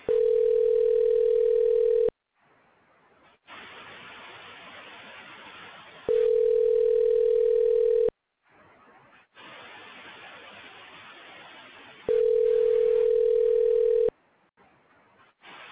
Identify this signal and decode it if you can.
Ringback tone (North America), 440 + 480 Hz.